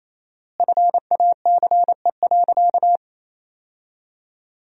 Morse code, heard as a keyed tone, about 27 wpm, 700 Hz. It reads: FACE.
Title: 4x Word-Spacing - Sentences from Top 300 Words - 28wpm